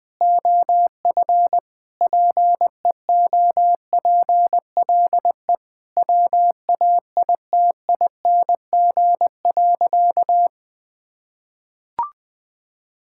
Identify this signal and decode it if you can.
Morse code, heard as a keyed tone, about 20 wpm, 700 Hz. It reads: OF PEOPLE WAITING.
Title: Sentences with 5-Word Phrases - Spoken Beginning - 20wpm